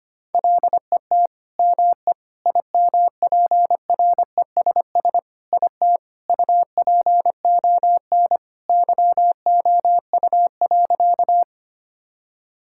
Morse code, heard as a keyed tone, about 25 wpm, 700 Hz. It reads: LET ME IMPRESS IT UPON YOU.
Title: Sentences from Top 1600 Words 25wpm